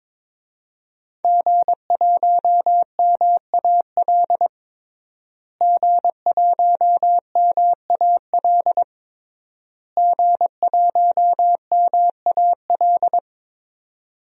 Morse code, heard as a keyed tone, about 23 wpm, 700 Hz. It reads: G1MAL G1MAL G1MAL
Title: DXCC-100 Call Signs - 5 Characters - Repeated 3x - 22wpm